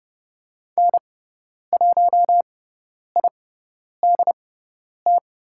Morse code, heard as a keyed tone, about 30 wpm, spaced as for 10 wpm, 690 Hz. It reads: N1IDT
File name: Lesson 17g - Letter u - Call Signs 30wpm